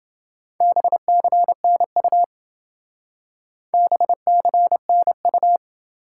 Morse code, heard as a keyed tone, about 28 wpm, 700 Hz. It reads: BCNU BCNU